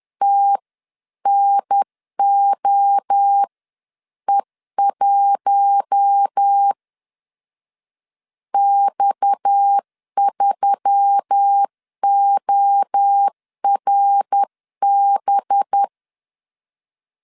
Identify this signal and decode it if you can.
Morse code, heard as a keyed tone, about 10 wpm, spaced as for 9 wpm, 800 Hz. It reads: TNOE1 X3ORB
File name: CW / DET / 1 6 1500C S C 10wpm 4252